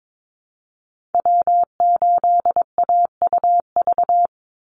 Morse code, heard as a keyed tone, about 22 wpm, 700 Hz. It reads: W8AUV
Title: USA Call Signs 22wpm